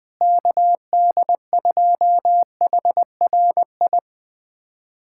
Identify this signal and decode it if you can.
Morse code, heard as a keyed tone, about 20 wpm, 700 Hz. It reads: KD2HRI
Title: USA Callsign DE Callsign 20wpm